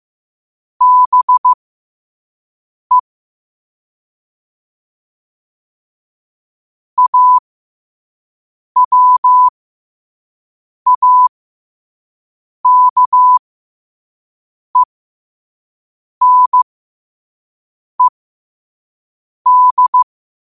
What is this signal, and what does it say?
Morse code, heard as a keyed tone, about 15 wpm, spaced as for 5 wpm, 1000 Hz. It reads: BE AWAKENED